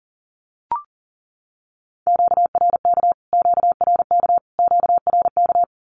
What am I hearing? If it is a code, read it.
Morse code, heard as a keyed tone, about 40 wpm, 700 Hz. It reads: QRK QRK QRK
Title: Q-Codes Repeated 3x 40wpm